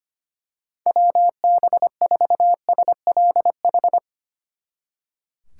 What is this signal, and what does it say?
Morse code, heard as a keyed tone, about 24 wpm, 700 Hz. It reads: WB4SLH
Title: Speed Racing - USA 2x3 Call Signs 25wpm